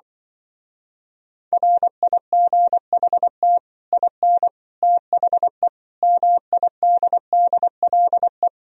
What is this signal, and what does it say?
Morse code, heard as a keyed tone, about 24 wpm, 700 Hz. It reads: RIGHT IN THE MIDDLE
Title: Speed-Racing - Sets of 5 Words 20wpm